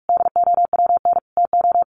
Morse code, heard as a keyed tone, about 44 wpm, 700 Hz. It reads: DOWN TO
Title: Sentences from Top 1500 Words 45wpm